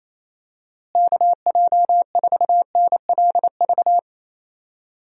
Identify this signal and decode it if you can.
Morse code, heard as a keyed tone, about 28 wpm, 700 Hz. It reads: KJ4NLV
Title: USA Call Signs - 6 Characters - Repeated 2x - 28wpm